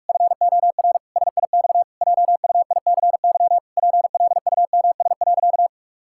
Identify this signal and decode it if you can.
Morse code, heard as a keyed tone, about 46 wpm, 700 Hz. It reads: FOR SIX JUICY PLUMS.
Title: Pangram Challenge v2 - 45wpm